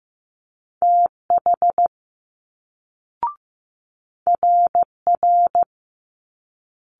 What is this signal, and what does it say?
Morse code, heard as a keyed tone, about 15 wpm, 700 Hz. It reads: TH RR
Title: Multi-letter Phonemes Quick Fire 15wpm